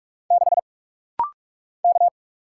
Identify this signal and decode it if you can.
Morse code, heard as a keyed tone, about 47 wpm, 700 Hz. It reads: 6 K
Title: Single Letter-Number - Rapid-Fire 45wpm